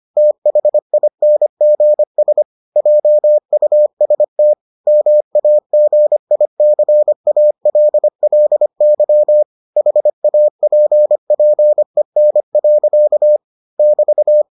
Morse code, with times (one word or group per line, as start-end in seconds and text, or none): 0.2-2.4 THINGS
2.8-4.5 JUST
4.9-9.4 MAGICALLY
9.8-13.4 HAPPEN.
13.8-14.4 <BT>